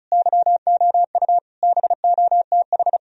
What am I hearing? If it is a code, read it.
Morse code, heard as a keyed tone, about 47 wpm, spaced as for 41 wpm, 700 Hz. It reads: YOU BOTH
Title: Speed-Racing - Sets of 5 Words 35wpm